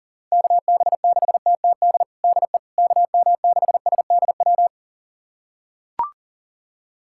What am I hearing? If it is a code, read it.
Morse code, heard as a keyed tone, about 40 wpm, 700 Hz. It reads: KB6TTD DE KM6SDW
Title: USA Callsign DE Callsign - 6 Characters - 40wpm